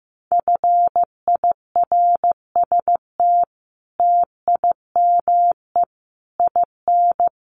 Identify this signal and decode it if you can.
Morse code, heard as a keyed tone, about 14 wpm, 700 Hz. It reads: FIRST TIME IN
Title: Sets of 5 Words Encore2 15wpm